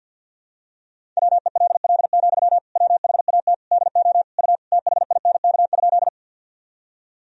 Morse code, heard as a keyed tone, about 49 wpm, 700 Hz. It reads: WELL, WHAT DO U THINK?
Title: Sentences from Top 200 Words and Abbreviations 50wpm